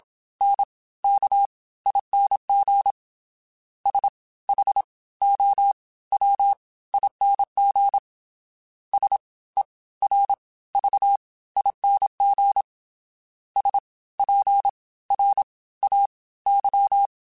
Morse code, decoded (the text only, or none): NKING SHOWING SERVING SPRAY